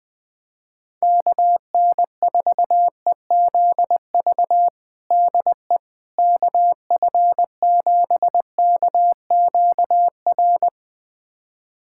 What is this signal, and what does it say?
Morse code, heard as a keyed tone, about 21 wpm, 700 Hz. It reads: KN4EZV DE KF7KQR
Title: USA Callsign DE Callsign 20wpm